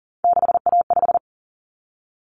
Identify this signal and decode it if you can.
Morse code, heard as a keyed tone, about 40 wpm, 700 Hz. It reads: 6A5
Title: Common Cut Numbers - 3-Digit 40wpm